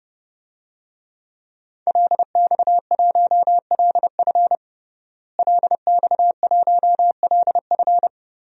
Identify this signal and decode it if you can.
Morse code, heard as a keyed tone, about 30 wpm, 700 Hz. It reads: LX1LF LX1LF
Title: DXCC-100 Call Signs - 5 Characters - Repeated 3x - 30wpm